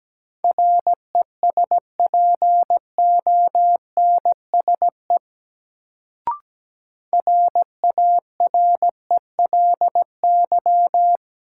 Morse code, heard as a keyed tone, about 17 wpm, 700 Hz. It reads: RESPONSE RARELY
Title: Sentences from Top 1400 Words - New Word Drill Alt - 17wpm